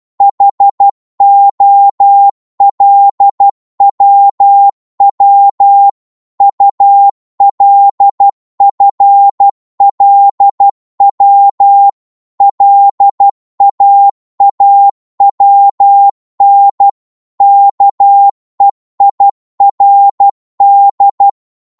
Morse code, heard as a keyed tone, about 12 wpm, 800 Hz. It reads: HOLWW ULFLW LAAWN KEIRD
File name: Morse / ltr-12wpm